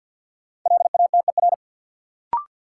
Morse code, heard as a keyed tone, about 51 wpm, 700 Hz. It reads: LATER
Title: Sentences from Top 700 Words - New Word Drill - 50wpm